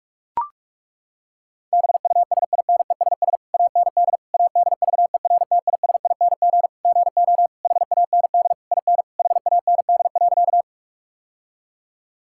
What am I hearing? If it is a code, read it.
Morse code, heard as a keyed tone, about 41 wpm, 700 Hz. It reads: BUSINESS AND ADVERTISING GO HAND IN HAND.
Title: Sentences from Top 1000 Words - Review Part 6 - 45wpm